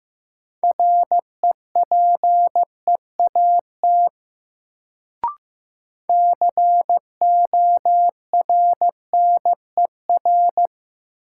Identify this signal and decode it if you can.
Morse code, heard as a keyed tone, about 15 wpm, 700 Hz. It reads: REPEAT CORNER
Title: Sentences from Top 1600 Words - New Word Drill Alt - 15wpm